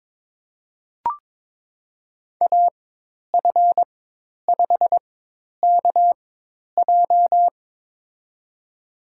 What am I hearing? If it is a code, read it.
Morse code, heard as a keyed tone, about 22 wpm, spaced as for 10 wpm, 700 Hz. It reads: AF5KJ
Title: USA Call Signs - 5 Characters - 4x Character Spacing - 22wpm